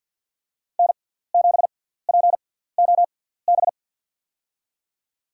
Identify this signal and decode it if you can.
Morse code, heard as a keyed tone, about 52 wpm, spaced as for 17 wpm, 700 Hz. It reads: N7POB
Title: USA Call Signs - 5 Characters - 6x Character Spacing - 50wpm